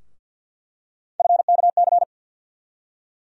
Morse code, heard as a keyed tone, about 51 wpm, 700 Hz. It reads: FKC
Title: USA Call Sign Postfixes - 3 Letters 50wpm